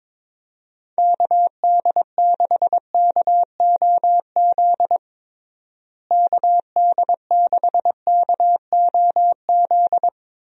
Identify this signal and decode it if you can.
Morse code, heard as a keyed tone, about 23 wpm, 700 Hz. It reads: KD6KOZ KD6KOZ
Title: USA Call Signs - 6 Characters - Repeated 2x - 22wpm